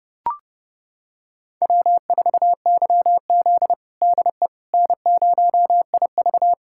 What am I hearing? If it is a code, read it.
Morse code, heard as a keyed tone, about 30 wpm, 700 Hz. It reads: W4YZ DE N0IV
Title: USA Callsign DE Callsign - 4 Characters - 30wpm